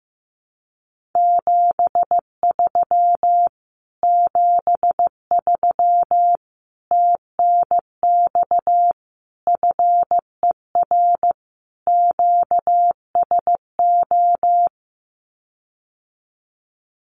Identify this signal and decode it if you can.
Morse code, heard as a keyed tone, about 15 wpm, 700 Hz. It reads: 73 73 TNX FER QSO
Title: Common QSO Phrases Encore 15wpm